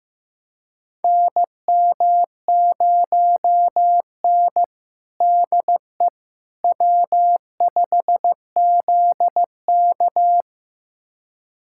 Morse code, heard as a keyed tone, about 15 wpm, 700 Hz. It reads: NM0N DE W5ZK
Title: USA Callsign DE Callsign - 4 Characters - 15wpm